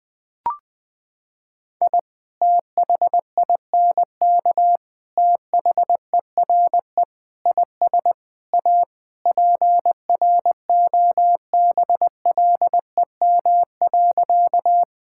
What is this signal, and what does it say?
Morse code, heard as a keyed tone, about 20 wpm, 700 Hz. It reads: I THINK THERE IS A PROBLEM.
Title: Sentences with 5-Word Phrases - 20wpm